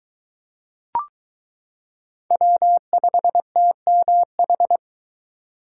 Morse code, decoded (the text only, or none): W5TMH